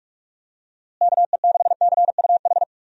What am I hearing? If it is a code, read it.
Morse code, heard as a keyed tone, about 46 wpm, 700 Hz. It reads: KE6CUH
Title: USA Call Signs - 6 Characters - Repeated 3x - 45wpm